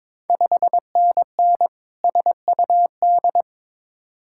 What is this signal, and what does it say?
Morse code, heard as a keyed tone, about 22 wpm, 700 Hz. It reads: 5NN SUD
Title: Ontario QSO Party Exchange 22wpm